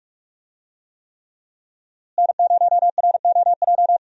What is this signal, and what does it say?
Morse code, heard as a keyed tone, about 44 wpm, 700 Hz. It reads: N0ROJ
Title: USA Call Signs - 5 Characters - Repeated 3x - 45wpm